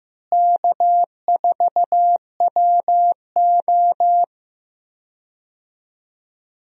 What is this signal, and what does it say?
Morse code, heard as a keyed tone, about 15 wpm, 700 Hz. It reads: K4WO
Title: USA Call Signs - 4 Characters - Repeated 2x - 15wpm